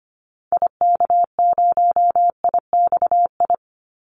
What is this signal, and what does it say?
Morse code, heard as a keyed tone, about 25 wpm, 700 Hz. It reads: IK0IXI
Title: Call Signs Repeated 3x - 25wpm